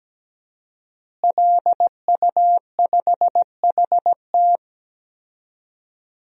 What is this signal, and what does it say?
Morse code, heard as a keyed tone, about 17 wpm, 700 Hz. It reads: LU5HT
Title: DXCC-100 Call Signs - 5 Characters - Repeated 2x - 17wpm